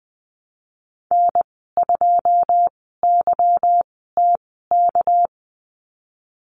Morse code, heard as a keyed tone, about 20 wpm, 700 Hz. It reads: N 2 Y T K
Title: 207 Lesson 33g - Letter j - Call Signs 20wpm